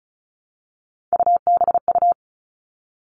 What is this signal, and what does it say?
Morse code, heard as a keyed tone, about 34 wpm, 700 Hz. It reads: UBU
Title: All Cut Numbers - 3-Digit 35wpm